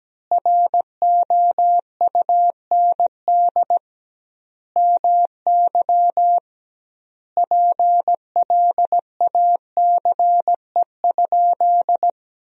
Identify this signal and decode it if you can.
Morse code, heard as a keyed tone, about 17 wpm, 700 Hz. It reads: ROUND MY PLACE?
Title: 2x Word-Spacing - Sentences from Top 200 Words - 17wpm